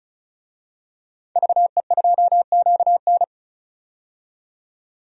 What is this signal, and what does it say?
Morse code, heard as a keyed tone, about 36 wpm, 700 Hz. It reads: VE2QN
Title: DXCC-100 Call Signs - 5 Characters - 35wpm